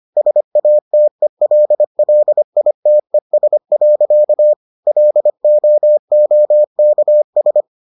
Morse code, heard as a keyed tone, about 25 wpm, 600 Hz. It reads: SATELLITES. LOOKS